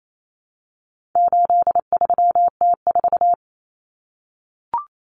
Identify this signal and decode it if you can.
Morse code, heard as a keyed tone, about 28 wpm, 700 Hz. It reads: 83T4
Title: Common Cut Numbers - 4-Digit 28wpm